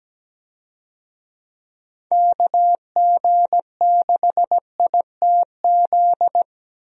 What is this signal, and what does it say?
Morse code, heard as a keyed tone, about 17 wpm, 700 Hz. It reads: KG6ITZ
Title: USA Call Signs - 6 Characters - Repeated 2x - 17wpm